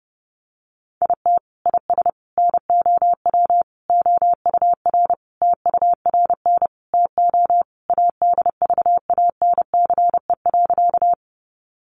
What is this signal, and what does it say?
Morse code, heard as a keyed tone, about 30 wpm, 700 Hz. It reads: IT IS NOW OUR TURN TO ADVANCE.
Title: Sentences from Top 1400 Words 30wpm